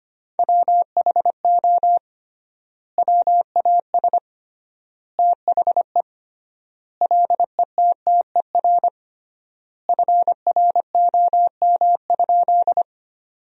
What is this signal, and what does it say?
Morse code, heard as a keyed tone, about 25 wpm, 700 Hz. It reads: WHO WAS THE LETTER FROM?
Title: 3x Word-Spacing - Sentences from Top 300 Words - 25wpm